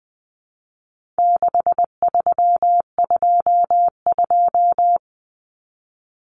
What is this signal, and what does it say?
Morse code, heard as a keyed tone, about 20 wpm, 700 Hz. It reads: 6322
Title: Numbers - 4-Digit 20wpm